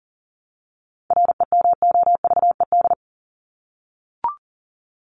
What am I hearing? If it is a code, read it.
Morse code, heard as a keyed tone, about 40 wpm, 700 Hz. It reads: REMOVED
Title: ED Suffix - Top 100 Words with Base Words 40wpm